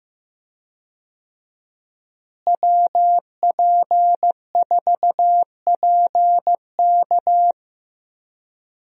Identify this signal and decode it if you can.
Morse code, heard as a keyed tone, about 15 wpm, 700 Hz. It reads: WP4PK
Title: USA Call Signs - 5 Characters - Repeated 2x - 15wpm